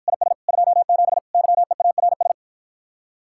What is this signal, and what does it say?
Morse code, heard as a keyed tone, about 54 wpm, 700 Hz. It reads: IS 27 YEARS